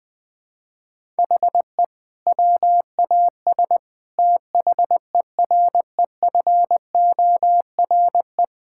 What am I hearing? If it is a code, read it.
Morse code, heard as a keyed tone, about 20 wpm, 700 Hz. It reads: HE WAS THEREFORE
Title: Sentences from Top 1000 Words - Review Part 5 - 20wpm